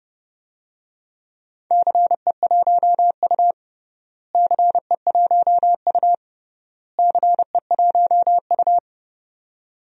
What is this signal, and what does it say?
Morse code, heard as a keyed tone, about 32 wpm, 700 Hz. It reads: CE1U CE1U CE1U
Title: DXCC-100 Call Signs - 4 Characters - Repeated 3x - 30wpm